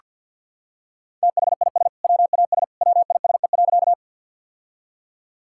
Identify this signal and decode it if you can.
Morse code, heard as a keyed tone, about 47 wpm, 700 Hz. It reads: THIS WAS WISE.
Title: Sentences from Top 1600 Words 50wpm